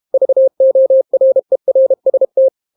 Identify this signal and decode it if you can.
Morse code, heard as a keyed tone, about 32 wpm, 530 Hz. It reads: VORERST